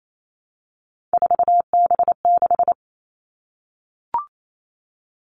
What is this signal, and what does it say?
Morse code, heard as a keyed tone, about 28 wpm, 700 Hz. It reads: 4B6
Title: Three Challenging Letter-Number Combinations 28wpm